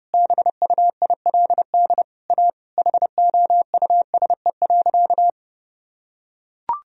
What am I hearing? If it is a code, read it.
Morse code, heard as a keyed tone, about 30 wpm, 700 Hz. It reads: BUILD A HOUSE.
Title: Sentences from Top 1000 Words - Review Part 4 - 30wpm